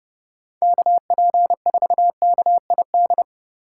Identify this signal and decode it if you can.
Morse code, heard as a keyed tone, about 30 wpm, 700 Hz. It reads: KP4KID